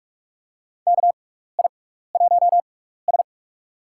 Morse code, heard as a keyed tone, about 45 wpm, spaced as for 15 wpm, 700 Hz. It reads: KI1S
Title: USA Call Signs - 4 Characters - 6x Character Spacing - 45wpm